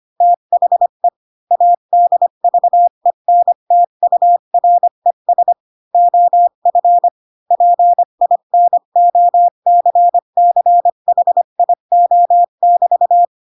Morse code, heard as a keyed tone, about 25 wpm, 700 Hz. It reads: THE ADVENTURES OF PINOCCHIO<BT>
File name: Pinocchio / Pinocchio ch3 20 25 ik1hge